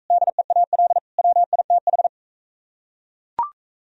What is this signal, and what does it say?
Morse code, heard as a keyed tone, about 38 wpm, 700 Hz. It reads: DEAL WITH